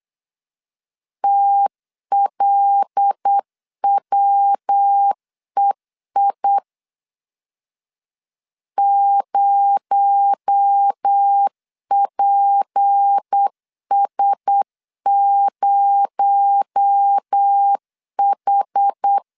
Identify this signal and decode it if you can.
Morse code, heard as a keyed tone, about 8 wpm, 800 Hz. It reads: TLWEI 0PS0H
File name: LEZIONI / 6 PKX0 / 6 PKX0 8wpm 3341